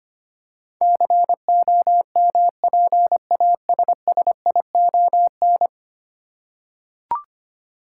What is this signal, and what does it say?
Morse code, heard as a keyed tone, about 25 wpm, 700 Hz. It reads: COMPASSION
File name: Ten Letter Words - 25wpm